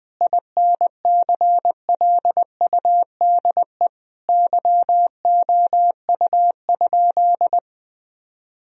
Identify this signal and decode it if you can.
Morse code, heard as a keyed tone, about 20 wpm, 700 Hz. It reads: INCLUDE YOU?
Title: Sentences from Top 1000 Words - Review Part 3 - 20wpm